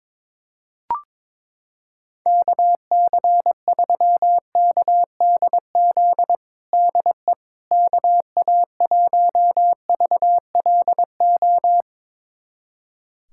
Morse code, heard as a keyed tone, about 21 wpm, 700 Hz. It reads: KC3KDZ DE KA1VLO
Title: USA Callsign DE Callsign - 6 Characters - 22wpm